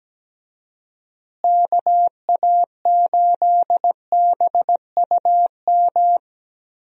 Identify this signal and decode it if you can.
Morse code, heard as a keyed tone, about 17 wpm, 700 Hz. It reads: KA8BUM
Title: USA Call Signs - 6 Characters - Repeated 3x - 17wpm